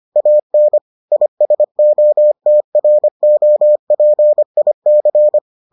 Morse code, heard as a keyed tone, about 25 wpm, 600 Hz. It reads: AN ISOTROPIC